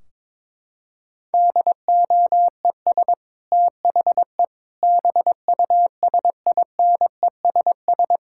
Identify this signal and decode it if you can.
Morse code, heard as a keyed tone, about 22 wpm, 700 Hz. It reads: DOES THE BUSINESS
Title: Sentences from Top 1600 Words 22wpm